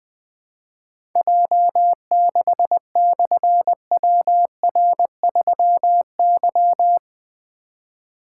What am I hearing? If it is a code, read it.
Morse code, heard as a keyed tone, about 20 wpm, 700 Hz. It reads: J6/WR3Y
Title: Difficult Call Signs 20wpm